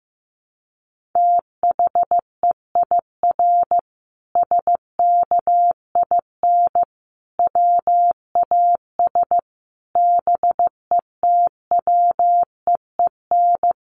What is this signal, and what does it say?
Morse code, heard as a keyed tone, about 15 wpm, 700 Hz. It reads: THEIR SKIN WAS BETWEEN
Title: Sentences from Top 1300 Words 15wpm